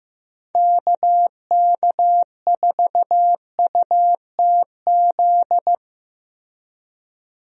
Morse code, heard as a keyed tone, about 15 wpm, 700 Hz. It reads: KK4UTZ
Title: USA Call Signs - 6 Characters - Repeated 2x - 15wpm